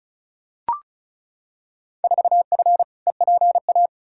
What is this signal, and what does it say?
Morse code, heard as a keyed tone, about 34 wpm, 700 Hz. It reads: 4F EPA
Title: Field Day Exchanges Encore - 35wpm